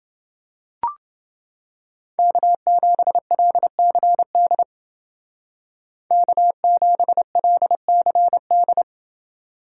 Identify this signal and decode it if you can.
Morse code, heard as a keyed tone, about 28 wpm, 700 Hz. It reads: K7LCD K7LCD